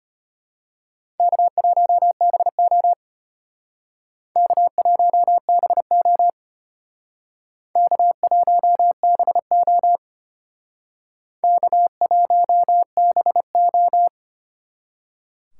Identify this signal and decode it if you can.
Morse code, heard as a keyed tone, about 32 wpm, 700 Hz. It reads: K1BO K1BO K1BO K1BO